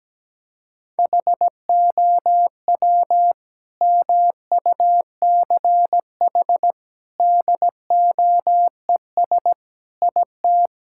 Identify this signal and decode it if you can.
Morse code, heard as a keyed tone, about 17 wpm, 700 Hz. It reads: HOW MUCH DOES IT
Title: Sentences from Top 1600 Words 17wpm